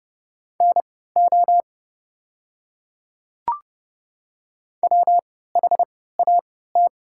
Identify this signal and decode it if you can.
Morse code, heard as a keyed tone, about 29 wpm, spaced as for 17 wpm, 690 Hz. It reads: NO WHAT
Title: 3x Character Spacing - Top 100 Words - 30wpm